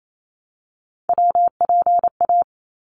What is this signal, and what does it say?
Morse code, heard as a keyed tone, about 28 wpm, 700 Hz. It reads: WPA
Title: ARRL Sections 28wpm